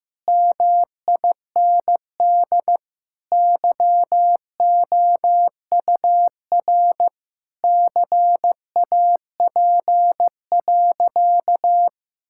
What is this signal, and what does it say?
Morse code, heard as a keyed tone, about 16 wpm, 700 Hz. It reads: MIND YOUR CAP.
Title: Sentences from Top 1600 Words 15wpm